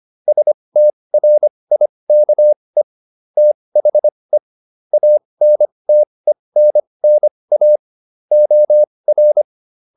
Morse code, heard as a keyed tone, about 25 wpm, spaced as for 20 wpm, 600 Hz. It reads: STRIKE THE ANTENNA OR